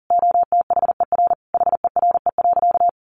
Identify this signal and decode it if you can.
Morse code, heard as a keyed tone, about 40 wpm, 700 Hz. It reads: OTHER HERE.
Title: Sentences from Top 1200 Words 40wpm